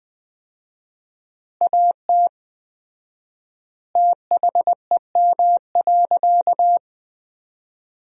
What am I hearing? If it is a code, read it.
Morse code, heard as a keyed tone, about 21 wpm, 700 Hz. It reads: AT THEM.
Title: 4x Word-Spacing - Sentences from Top 100 Words - 20wpm